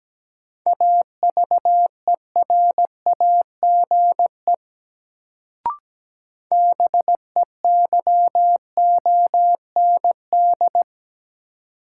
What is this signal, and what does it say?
Morse code, heard as a keyed tone, about 17 wpm, 700 Hz. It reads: AVERAGE BEYOND
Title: Sentences from Top 1000 Words - Review Part 6 - New Word Drill - 17wpm